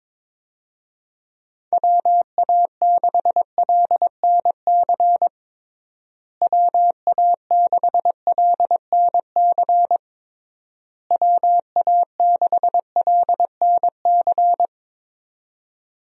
Morse code, heard as a keyed tone, about 21 wpm, 700 Hz. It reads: WA6LNC WA6LNC WA6LNC
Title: USA Call Signs - 6 Characters - Repeated 3x - 22wpm